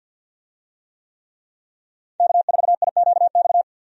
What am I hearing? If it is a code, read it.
Morse code, heard as a keyed tone, about 48 wpm, 700 Hz. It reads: K4IQX
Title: USA Call Signs - 5 Characters - 50wpm